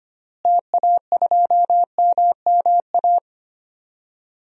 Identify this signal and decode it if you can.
Morse code, heard as a keyed tone, about 25 wpm, 700 Hz. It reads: TA2MMA
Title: DXCC-100 Call Signs - 6 Characters - Repeated 2x - 25wpm